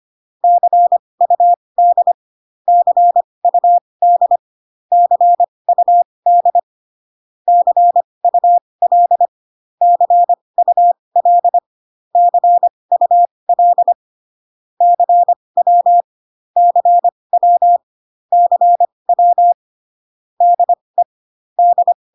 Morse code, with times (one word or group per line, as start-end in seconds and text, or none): 0.4-2.1 CUD
2.7-4.4 CUD
4.9-6.6 CUD
7.5-9.3 CUL
9.8-11.6 CUL
12.2-13.9 CUL
14.8-16.0 CW
16.6-17.8 CW
18.3-19.5 CW
20.4-21.0 DE
21.6-21.9 D